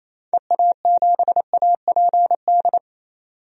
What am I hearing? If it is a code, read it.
Morse code, heard as a keyed tone, about 28 wpm, 700 Hz. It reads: EA7APD
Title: DXCC-100 Call Signs - 6 Characters - 28wpm